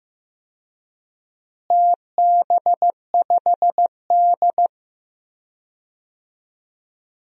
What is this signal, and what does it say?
Morse code, heard as a keyed tone, about 15 wpm, 700 Hz. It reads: TB5D
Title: DXCC-100 Call Signs - 4 Characters - Repeated 3x - 15wpm